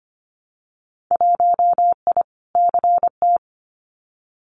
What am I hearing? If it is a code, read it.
Morse code, heard as a keyed tone, about 25 wpm, 700 Hz. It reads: 1I CT
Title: Winter Field Day Exchanges 25wpm